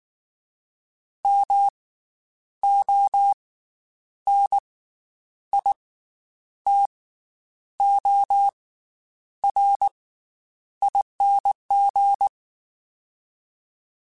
Morse code, heard as a keyed tone, about 19 wpm, spaced as for 9 wpm, 800 Hz. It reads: MONITORING
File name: Special INGs / Special INGs-mp3s / Slow ing @18